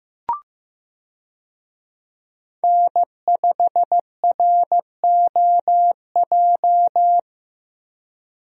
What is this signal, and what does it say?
Morse code, heard as a keyed tone, about 15 wpm, 700 Hz. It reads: N5ROJ
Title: USA Call Signs - 5 Characters - Repeated 2x - 15wpm